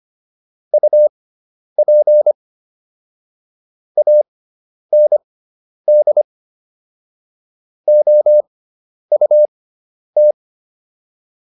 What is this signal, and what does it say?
Morse code, heard as a keyed tone, about 25 wpm, spaced as for 10 wpm, 600 Hz. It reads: UP AND OUT